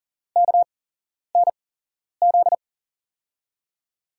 Morse code, heard as a keyed tone, about 40 wpm, spaced as for 11 wpm, 700 Hz. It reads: KNZ
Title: USA Call Signs - 6 Characters - 8x Character Spacing - 40wpm